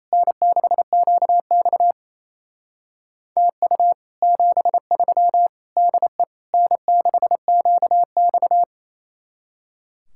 Morse code, heard as a keyed tone, about 34 wpm, 700 Hz. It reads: N6QX TU 73 DE N6QX
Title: Speed-Racing Common QSO Phrases - 28wpm